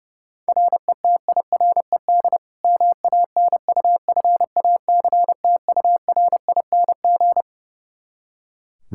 Morse code, heard as a keyed tone, about 31 wpm, 700 Hz. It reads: RETIRED MANUFACTURING
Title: K1USN SST QSO Practice 30wpm